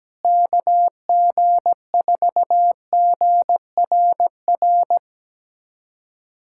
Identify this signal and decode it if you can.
Morse code, heard as a keyed tone, about 17 wpm, 700 Hz. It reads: KG4GRR
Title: USA Callsign DE Callsign - 6 Characters - 17wpm